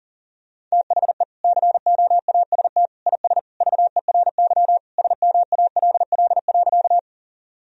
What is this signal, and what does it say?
Morse code, heard as a keyed tone, about 37 wpm, 700 Hz. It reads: THE COAST IS VERY SMALL.
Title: Sentences from Top 1000 Words - Review Part 4 - 40wpm